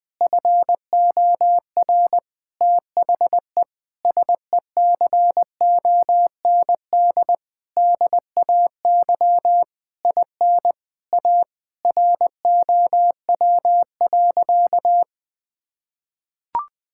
Morse code, heard as a keyed tone, about 20 wpm, 700 Hz. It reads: FOR THE SECOND DAY IN A ROW.
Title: Sentences with 7-Word Phrases - Spoken Beginning - 20wpm